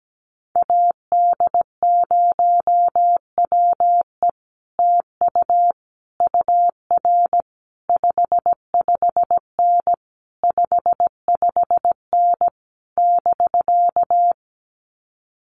Morse code, heard as a keyed tone, about 17 wpm, 700 Hz. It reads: AD0WE TU UR 55N 55N <BK>
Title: Common QSO Phrases Encore 17wpm